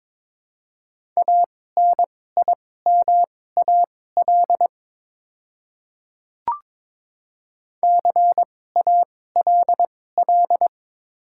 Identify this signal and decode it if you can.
Morse code, heard as a keyed tone, about 22 wpm, spaced as for 16 wpm, 690 Hz. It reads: ANIMAL CALL
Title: 2x Character Spacing - Top 300 Words - 22wpm